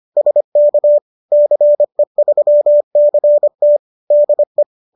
Morse code, heard as a keyed tone, about 25 wpm, 590 Hz. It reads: SK CE3CT DE